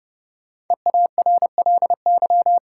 Morse code, heard as a keyed tone, about 31 wpm, 700 Hz. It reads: EARLY
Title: Sentences from Top 400 Words - New Word Drill - 30wpm